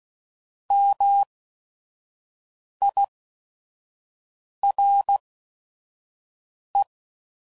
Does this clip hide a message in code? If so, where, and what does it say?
Yes: Morse code from 0.7 to 6.8 s, heard as a keyed tone, about 16 wpm, spaced as for 5 wpm, 800 Hz. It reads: MIRE